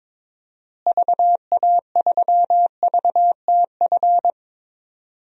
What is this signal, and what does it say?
Morse code, heard as a keyed tone, about 22 wpm, 700 Hz. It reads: VA3VTF
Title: DXCC-100 Call Signs - 6 Characters - Repeated 2x - 22wpm